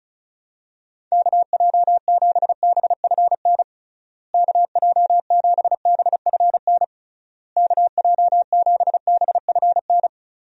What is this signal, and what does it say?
Morse code, heard as a keyed tone, about 36 wpm, 700 Hz. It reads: KJ7BFN KJ7BFN KJ7BFN
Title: USA Call Signs - 6 Characters - Repeated 3x - 35wpm